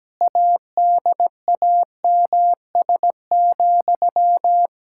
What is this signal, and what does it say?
Morse code, heard as a keyed tone, about 17 wpm, 700 Hz. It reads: ADAMS,
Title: Practice QSO - Exchange 1 - Beginning - 17wpm